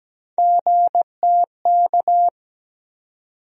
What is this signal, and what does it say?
Morse code, heard as a keyed tone, about 17 wpm, 700 Hz. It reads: GTK
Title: USA Call Sign Postfixes - 3 Letters 17wpm